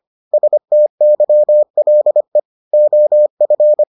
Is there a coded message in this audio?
Yes: Morse code from 0.3 to 3.8 s, heard as a keyed tone, about 25 wpm, 600 Hz. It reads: STYLE OF